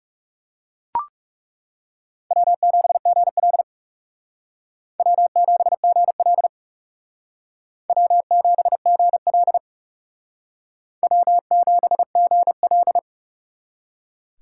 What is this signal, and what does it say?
Morse code, heard as a keyed tone, about 37 wpm, 700 Hz. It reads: W7GL W7GL W7GL W7GL
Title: Speed Racing - USA 1x2 Call Signs 30wpm